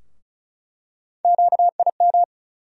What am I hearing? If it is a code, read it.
Morse code, heard as a keyed tone, about 34 wpm, 700 Hz. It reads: QIM
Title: USA Call Sign Postfixes - 3 Letters 35wpm